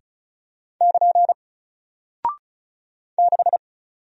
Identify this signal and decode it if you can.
Morse code, heard as a keyed tone, about 36 wpm, 700 Hz. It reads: <KN> 6
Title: Single Character (Letter:Number:Punctuation:Prosign) Alt 35wpm